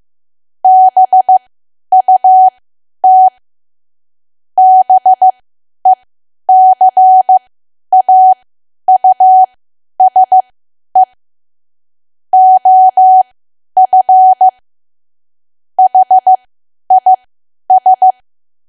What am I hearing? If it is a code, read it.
Morse code, heard as a keyed tone, about 15 wpm, spaced as for 10 wpm, 750 Hz. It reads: BUT BECAUSE OF HIS
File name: cwa / W1AW-Level2 / 10WPM